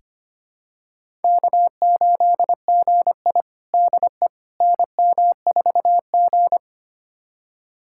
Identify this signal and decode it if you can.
Morse code, heard as a keyed tone, about 25 wpm, 700 Hz. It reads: K8GI DE NM4G
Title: USA Callsign DE Callsign - 4 Characters - 25wpm